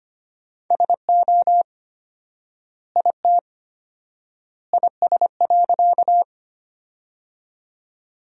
Morse code, heard as a keyed tone, about 25 wpm, 700 Hz. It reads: SO IT IS.
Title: 4x Word-Spacing - Sentences from Top 100 Words - 25wpm